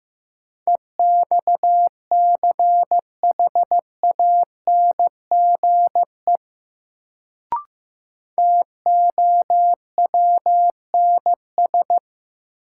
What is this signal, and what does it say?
Morse code, heard as a keyed tone, about 15 wpm, 700 Hz. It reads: EXCHANGE TOWNS
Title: Sentences from Top 1000 Words - Review Part 6 - New Word Drill - 15wpm